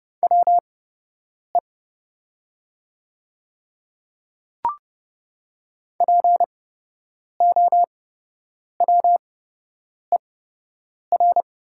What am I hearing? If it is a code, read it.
Morse code, heard as a keyed tone, about 30 wpm, spaced as for 8 wpm, 700 Hz. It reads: WE POWER
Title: 8x Character Spacing - Top 500 Words - 30wpm